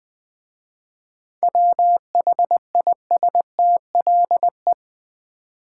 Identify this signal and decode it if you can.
Morse code, heard as a keyed tone, about 20 wpm, 700 Hz. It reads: WHISTLE
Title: Easy Words with Hints - 20wpm